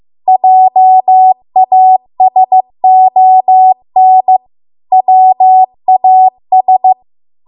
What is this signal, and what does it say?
Morse code, heard as a keyed tone, about 15 wpm, 750 Hz. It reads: JASON WAS